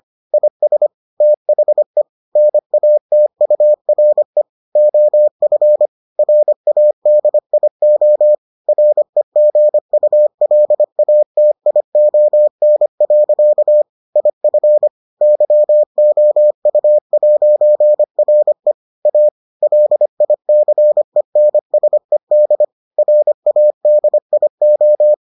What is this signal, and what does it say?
Morse code, heard as a keyed tone, about 25 wpm, 600 Hz. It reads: IS THE NATURE OF RADIO REGULATION. IF YOU'RE A LICENSED RADIO